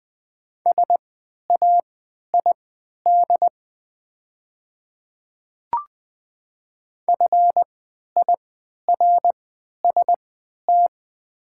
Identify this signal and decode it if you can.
Morse code, heard as a keyed tone, about 20 wpm, spaced as for 11 wpm, 690 Hz. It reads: SAID FIRST
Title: 3x Character Spacing - Top 100 Words - 20wpm